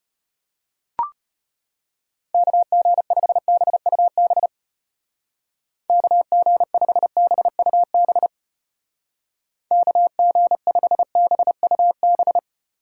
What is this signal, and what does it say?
Morse code, heard as a keyed tone, about 32 wpm, 700 Hz. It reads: KG5BUB KG5BUB KG5BUB